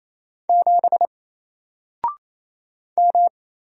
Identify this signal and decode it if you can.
Morse code, heard as a keyed tone, about 28 wpm, 700 Hz. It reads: 7 M
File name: Single-Letter-Number-28wpm